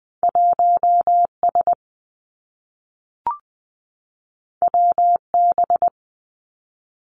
Morse code, heard as a keyed tone, about 19 wpm, 700 Hz. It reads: 1 S W B
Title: Two Challenging Letter-Number Combinations 20wpm